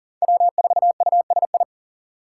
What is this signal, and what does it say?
Morse code, heard as a keyed tone, about 40 wpm, 700 Hz. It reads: W4USI
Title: K1USN SST QSO Practice - 40wpm